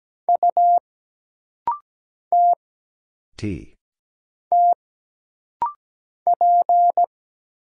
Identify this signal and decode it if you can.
Morse code, heard as a keyed tone, about 17 wpm, 700 Hz. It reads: U T T P
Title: Single Letter-Number Alt - Rapid-Fire 17wpm